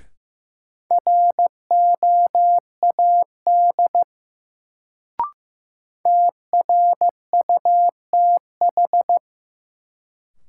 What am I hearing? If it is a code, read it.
Morse code, heard as a keyed tone, about 15 wpm, 700 Hz. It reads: ROAD TRUTH
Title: Sets of 5 Words - New Word Drill - 15wpm